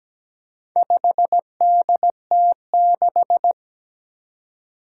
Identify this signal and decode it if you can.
Morse code, heard as a keyed tone, about 17 wpm, 700 Hz. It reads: 5DT6
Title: Four Challenging Letter-Number Combinations 17wpm